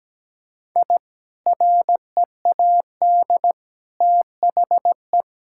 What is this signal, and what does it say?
Morse code, heard as a keyed tone, about 17 wpm, 700 Hz. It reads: I READ THE
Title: Sentences from Top 1000 Words - Review Part 6 - 17wpm